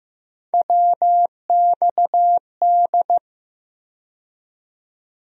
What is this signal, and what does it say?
Morse code, heard as a keyed tone, about 15 wpm, 700 Hz. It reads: WXD
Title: USA Call Sign Postfixes - 3 Letters 15wpm